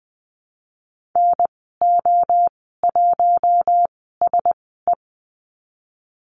Morse code, heard as a keyed tone, about 20 wpm, spaced as for 15 wpm, 700 Hz. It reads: NO1SE
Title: 048 Lesson 9g - Number 4 - Call Signs 20wpm